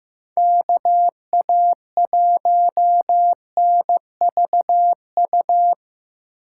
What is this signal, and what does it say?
Morse code, heard as a keyed tone, about 18 wpm, 700 Hz. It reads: KA1NVU
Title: Speed Racing - USA 2x3 Call Signs 15wpm